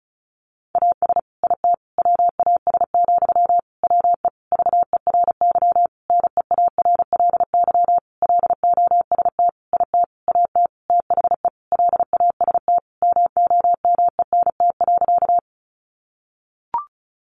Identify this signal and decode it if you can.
Morse code, heard as a keyed tone, about 34 wpm, 700 Hz. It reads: AS IT WAS, WE VERY NEARLY LOST IT AT THE LAST MOMENT.
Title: Sentences from Top 1400 Words 35wpm